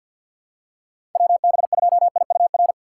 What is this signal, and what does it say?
Morse code, heard as a keyed tone, about 50 wpm, 700 Hz. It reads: WB2IUR
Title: USA Call Signs - 6 Characters - 3x Character Spacing - 50wpm